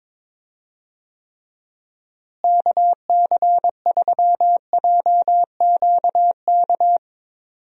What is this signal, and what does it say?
Morse code, heard as a keyed tone, about 22 wpm, 700 Hz. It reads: KC3JQK
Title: USA Call Signs - 6 Characters - Repeated 3x - 22wpm